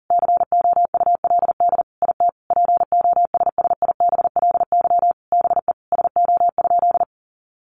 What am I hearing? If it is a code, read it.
Morse code, heard as a keyed tone, about 40 wpm, 700 Hz. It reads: COULD IT POSSIBLY BE SO?
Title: Sentences from Top 1300 Words 40wpm